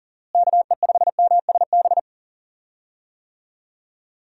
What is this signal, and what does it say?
Morse code, heard as a keyed tone, about 40 wpm, 700 Hz. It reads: KE5MSB
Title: USA Call Signs - 6 Characters - Repeated 3x - 40wpm